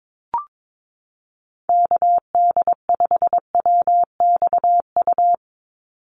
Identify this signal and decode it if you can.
Morse code, heard as a keyed tone, about 22 wpm, 700 Hz. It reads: KD5WXU
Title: USA Call Signs 22wpm